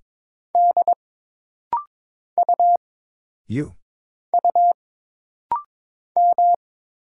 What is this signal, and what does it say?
Morse code, heard as a keyed tone, about 22 wpm, 700 Hz. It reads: D U U M
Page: This rapid-fire timing is geared to push your ICR (Instant Character Recognition) skills.